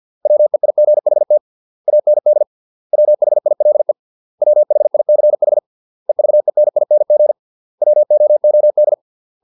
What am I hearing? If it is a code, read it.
Morse code, heard as a keyed tone, about 50 wpm, 600 Hz. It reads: WEIGHT AND WHILE WHICH EVENING WOOD